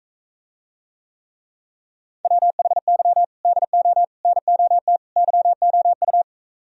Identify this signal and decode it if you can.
Morse code, heard as a keyed tone, about 44 wpm, 700 Hz. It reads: WHY DO NOT YOU